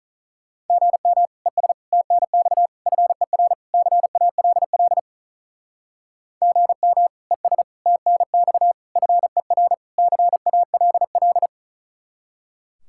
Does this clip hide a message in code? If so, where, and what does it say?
Yes: Morse code from 0.7 to 11.5 s, heard as a keyed tone, about 45 wpm, 700 Hz. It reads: GM ES TNX FER CALL GM ES TNX FER CALL